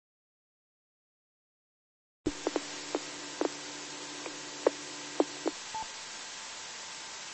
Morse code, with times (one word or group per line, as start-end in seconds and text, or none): none